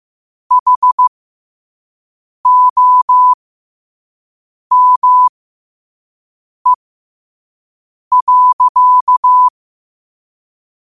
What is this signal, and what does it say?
Morse code, heard as a keyed tone, about 15 wpm, spaced as for 5 wpm, 1000 Hz. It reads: HOME.
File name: qotdfiles / 5wpm